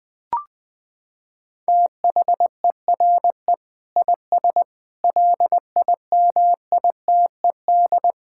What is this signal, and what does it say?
Morse code, heard as a keyed tone, about 20 wpm, 700 Hz. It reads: THERE IS LIMITED
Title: Sentences from Top 1000 Words - Review Part 6 - 20wpm